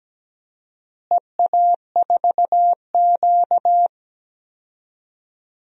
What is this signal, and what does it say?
Morse code, heard as a keyed tone, about 17 wpm, 700 Hz. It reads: EA4Q
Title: DXCC-100 Call Signs - 4 Characters - Repeated 2x - 17wpm